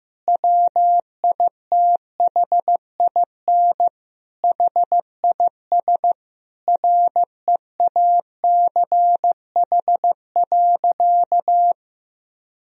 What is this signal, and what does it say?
Morse code, heard as a keyed tone, about 15 wpm, 700 Hz. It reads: WITHIN HIS REACH.
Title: Sentences from Top 1000 Words - Review Part 4 - 15wpm